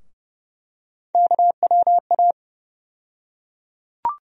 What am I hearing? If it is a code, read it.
Morse code, heard as a keyed tone, about 30 wpm, 700 Hz. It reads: KWA
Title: USA Call Sign Postfixes - 3 Letters 30wpm